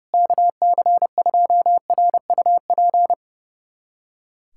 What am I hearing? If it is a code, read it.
Morse code, heard as a keyed tone, about 30 wpm, 700 Hz. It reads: KC2RUP
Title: USA Callsign DE Callsign - 6 Characters - 30wpm